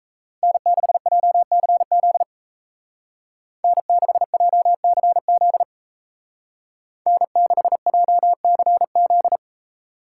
Morse code, heard as a keyed tone, about 33 wpm, 700 Hz. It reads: N6JCZ N6JCZ N6JCZ
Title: Speed Racing - USA 1x3 Call Signs 28wpm